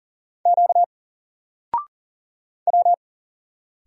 Morse code, heard as a keyed tone, about 40 wpm, 700 Hz. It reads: Q W
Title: Single Letter-Number 40wpm